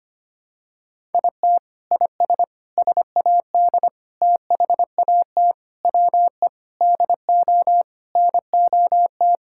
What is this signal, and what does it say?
Morse code, heard as a keyed tone, about 25 wpm, 700 Hz. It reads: IT IS SAD THAT WE DO NOT
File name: practice / 25wpm / Sentences with 4-Word Phrases - 25wpm